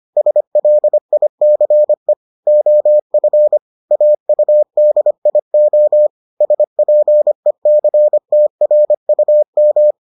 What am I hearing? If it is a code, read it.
Morse code, heard as a keyed tone, about 25 wpm, 600 Hz. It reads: SLICE OF AUDIO SPECTRUM